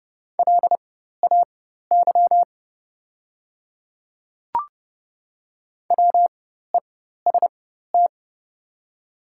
Morse code, heard as a keyed tone, about 30 wpm, spaced as for 14 wpm, 700 Hz. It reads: LAY WEST
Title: 4x Character Spacing - Top 500 Words - 30wpm